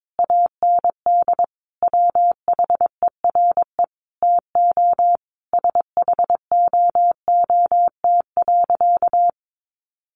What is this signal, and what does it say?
Morse code, heard as a keyed tone, about 22 wpm, 700 Hz. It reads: AND WHERE TO SHOOT.
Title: A Sound Of Thunder 22wpm